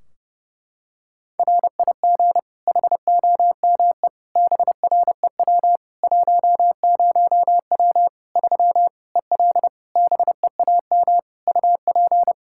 Morse code, heard as a keyed tone, about 30 wpm, 700 Hz. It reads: RIG HOME BREW 10W 3 EL BEAM UP
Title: K1USN SST QSO Practice 30wpm